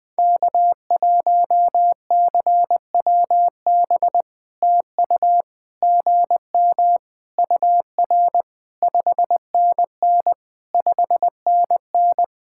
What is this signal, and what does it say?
Morse code, heard as a keyed tone, about 20 wpm, 690 Hz. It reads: K1CWB TU GM UR 5NN 5NN
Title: POTA QSO Practice - 20wpm